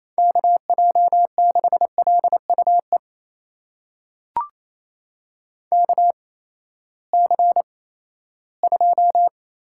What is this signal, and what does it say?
Morse code, heard as a keyed tone, about 28 wpm, spaced as for 16 wpm, 700 Hz. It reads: KJ6LUE KC2
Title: USA Call Signs - 6 Characters - 8x Character Spacing - 28wpm